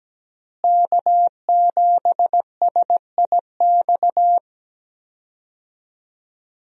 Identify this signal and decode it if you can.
Morse code, heard as a keyed tone, about 17 wpm, 700 Hz. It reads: K7SIX
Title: USA Call Signs - 5 Characters - Repeated 2x - 17wpm